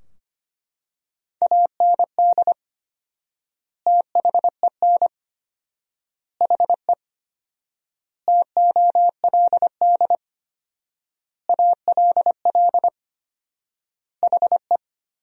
Morse code, heard as a keyed tone, about 25 wpm, 700 Hz. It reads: AND THEN HE TOLD ALL HE
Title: 4x Word-Spacing - Sentences from Top 400 Words - 25wpm